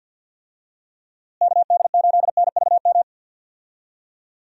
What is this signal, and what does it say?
Morse code, heard as a keyed tone, about 52 wpm, 700 Hz. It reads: KD8NVM
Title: USA Call Signs - 6 Characters - Repeated 2x - 50wpm